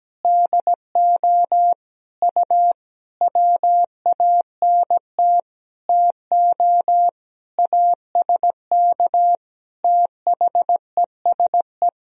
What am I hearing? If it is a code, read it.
Morse code, heard as a keyed tone, about 17 wpm, 700 Hz. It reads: DO U WANT TO ASK THESE
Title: Sentences from Top 200 Words and Abbreviations 17wpm